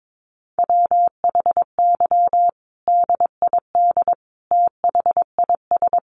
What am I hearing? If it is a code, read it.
Morse code, heard as a keyed tone, about 22 wpm, 700 Hz. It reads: WHY DID THIS
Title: Sentences from Top 900 Words 22wpm